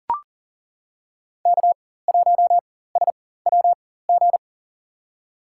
Morse code, heard as a keyed tone, about 40 wpm, spaced as for 19 wpm, 700 Hz. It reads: K1SWG
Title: USA Call Signs - 5 Characters - 4x Character Spacing - 40wpm